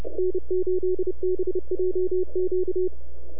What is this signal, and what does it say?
Morse code, heard as a keyed tone, about 29 wpm, 370 Hz. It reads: N8BJQ
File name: NAQPCW-8-2009 / W1UJ NAQP CW 09 2230Z